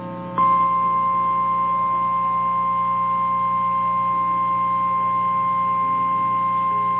Emergency Alert System (SAME) tones (the 1050 Hz attention tone), with noise. It is heard from 0.4 to 6.1 s.
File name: eas / KNCYKBIE / KNCYKBIE